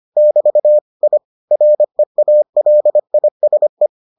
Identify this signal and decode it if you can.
Morse code, heard as a keyed tone, about 25 wpm, 600 Hz. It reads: <BT> I REALISE